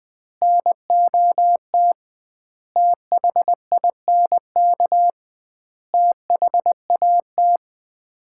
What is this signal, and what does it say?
Morse code, heard as a keyed tone, about 20 wpm, 700 Hz. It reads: NOT THINK THAT
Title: 2x Word-Spacing - Sets of 5 Words - 20wpm